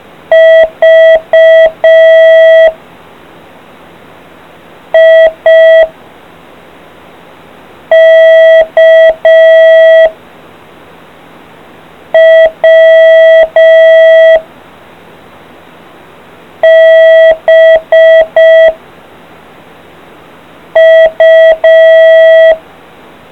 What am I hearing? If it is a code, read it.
Morse code, heard as a keyed tone, about 5 wpm, 650 Hz. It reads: V I K W B U